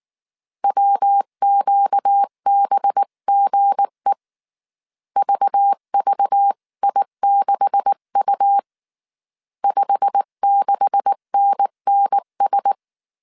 Morse code, heard as a keyed tone, about 19 wpm, 800 Hz. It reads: WQBGE VVI6U 56NNS
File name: CW / DET / 1 9 1500C S C 18wpm 2539